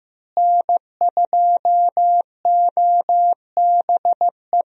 Morse code, heard as a keyed tone, about 15 wpm, 700 Hz. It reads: N2OBE
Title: USA Call Signs - 5 Characters - Repeated 3x - 15wpm